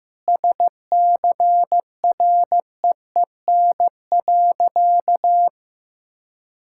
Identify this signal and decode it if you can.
Morse code, heard as a keyed tone, about 15 wpm, 700 Hz. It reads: SCREEN.
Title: Sentences from Top 1000 Words - Review Part 6 - 15wpm